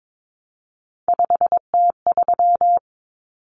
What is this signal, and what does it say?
Morse code, heard as a keyed tone, about 22 wpm, 700 Hz. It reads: 5T3
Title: Common Cut Numbers - 3-Digit 22wpm